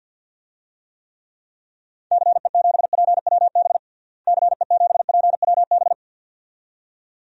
Morse code, heard as a keyed tone, about 53 wpm, 700 Hz. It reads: CE7PWB CE7PWB
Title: DXCC-100 Call Signs - 6 Characters - Repeated 2x - 50wpm